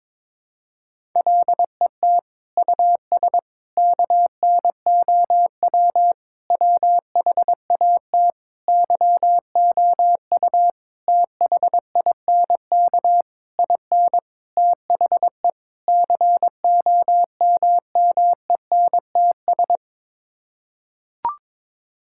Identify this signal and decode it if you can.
Morse code, heard as a keyed tone, about 22 wpm, 700 Hz. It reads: LET US KNOW WHAT YOU THINK IN THE COMMENTS
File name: Sets of 9 Words - 22wpm